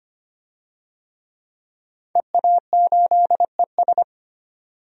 Morse code, heard as a keyed tone, about 25 wpm, 700 Hz. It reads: EA8ES
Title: DXCC-100 Call Signs - 5 Characters - 25wpm